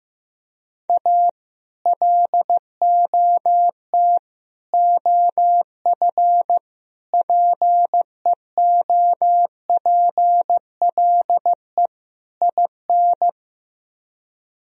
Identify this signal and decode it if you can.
Morse code, heard as a keyed tone, about 15 wpm, 700 Hz. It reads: A LOT OF PEOPLE IN
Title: Speed-Racing - Sets of 5 Words 15wpm